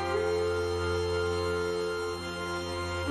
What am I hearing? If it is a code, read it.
Dial tone (North America), 350 + 440 Hz, with noise.
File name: wrct-log hi- 03